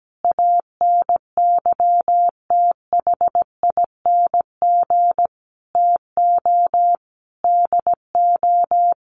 Morse code, heard as a keyed tone, about 17 wpm, 700 Hz. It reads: ANYTHING TO DO
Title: Sentences from Top 1000 Words 17wpm